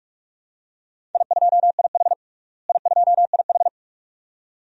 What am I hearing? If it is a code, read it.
Morse code, heard as a keyed tone, about 46 wpm, 700 Hz. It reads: I2IH I2IH
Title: DXCC-100 Call Signs - 4 Characters - Repeated 2x - 45wpm